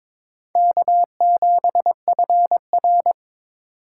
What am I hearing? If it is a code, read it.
Morse code, heard as a keyed tone, about 26 wpm, 700 Hz. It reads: K7FR
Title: Speed Racing - USA 1x2 Call Signs 22wpm